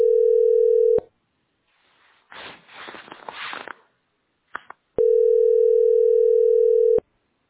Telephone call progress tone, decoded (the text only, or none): ringback tone